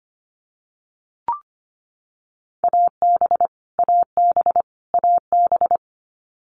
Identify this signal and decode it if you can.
Morse code, heard as a keyed tone, about 25 wpm, 700 Hz. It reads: AB AB AB
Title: Canadian Provinces and Territories - Repeated 3x - 25wpm